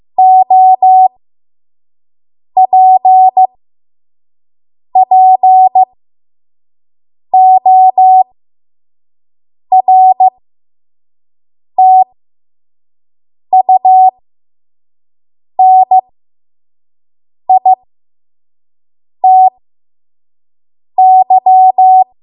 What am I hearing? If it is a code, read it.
Morse code, heard as a keyed tone, about 15 wpm, spaced as for 5 wpm, 750 Hz. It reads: OPPORTUNITY
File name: Morse / 05WPM